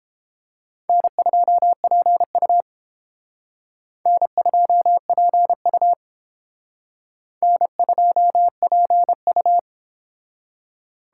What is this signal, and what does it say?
Morse code, heard as a keyed tone, about 30 wpm, 700 Hz. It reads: N2PU N2PU N2PU